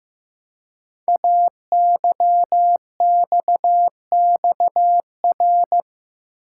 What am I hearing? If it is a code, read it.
Morse code, heard as a keyed tone, about 15 wpm, 700 Hz. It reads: AYXXR
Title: Code Groups - 5 Letters - 15wpm